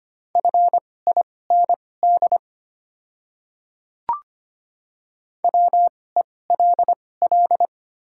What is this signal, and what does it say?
Morse code, heard as a keyed tone, about 25 wpm, spaced as for 18 wpm, 700 Hz. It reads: FIND WELL
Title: 2x Character Spacing - Top 300 Words - 25wpm